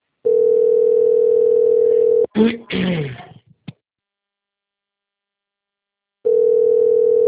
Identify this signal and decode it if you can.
Ringback tone (North America), 440 + 480 Hz, with speech.
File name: out- - - - - .23660